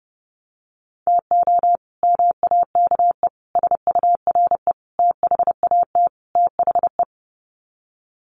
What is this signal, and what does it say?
Morse code, heard as a keyed tone, about 30 wpm, 700 Hz. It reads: TO MAKE SURE THAT THE
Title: Sets of 5 Words 30wpm